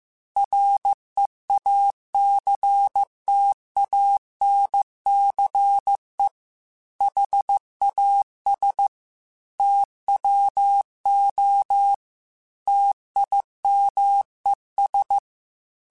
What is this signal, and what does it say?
Morse code, heard as a keyed tone, about 15 wpm, 800 Hz. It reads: REACTANCE HAS TWO TIMES
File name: Technical Subjects / Technical Subjects-mp3s / XL @13